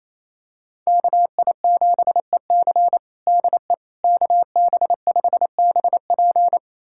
Morse code, heard as a keyed tone, about 28 wpm, 700 Hz. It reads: KI7EC DE KB5BP
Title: USA Callsign DE Callsign - 5 Characters - 28wpm